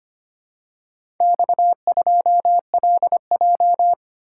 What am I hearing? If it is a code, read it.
Morse code, heard as a keyed tone, about 25 wpm, 700 Hz. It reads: X2LJ
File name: practice / 25wpm / Four Letter-Number Combinations - 25wpm